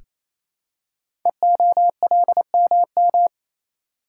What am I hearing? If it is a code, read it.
Morse code, heard as a keyed tone, about 29 wpm, 700 Hz. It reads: EOLMM
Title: Code Groups - 5 Letters - 28wpm